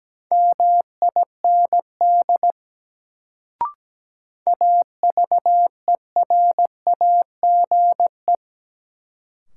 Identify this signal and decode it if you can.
Morse code, heard as a keyed tone, about 17 wpm, 700 Hz. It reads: MIND AVERAGE
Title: Sets of 5 Words - New Word Drill - 17wpm